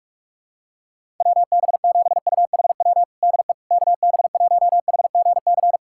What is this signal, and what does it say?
Morse code, heard as a keyed tone, about 46 wpm, 690 Hz. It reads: WB7UHW DE KB1HGC
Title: USA Callsign DE Callsign 45wpm